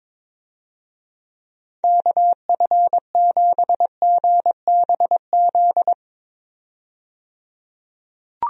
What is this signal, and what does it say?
Morse code, heard as a keyed tone, about 22 wpm, 700 Hz. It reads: KF7GBZ
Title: USA Call Signs - 6 Characters - 22wpm